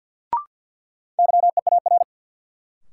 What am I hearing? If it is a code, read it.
Morse code, heard as a keyed tone, about 46 wpm, 700 Hz. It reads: YEAR
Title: Sentences from Top 200 Words - New Word Drill - 50wpm